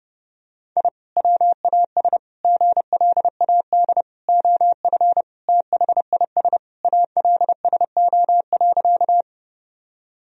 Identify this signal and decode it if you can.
Morse code, heard as a keyed tone, about 30 wpm, 700 Hz. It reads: I WAS GLAD OF THIS ALSO.
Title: Sentences from Top 1000 Words - Review Part 1 - 30wpm